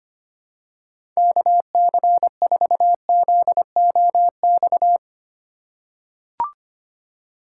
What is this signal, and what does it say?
Morse code, heard as a keyed tone, about 25 wpm, 700 Hz. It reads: KC4ZOX
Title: USA Call Signs - 6 Characters - 3x Character Spacing - 25wpm